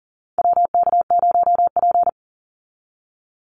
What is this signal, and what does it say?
Morse code, heard as a keyed tone, about 40 wpm, 700 Hz. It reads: WK0P
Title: Call Signs Repeated 2x - 40wpm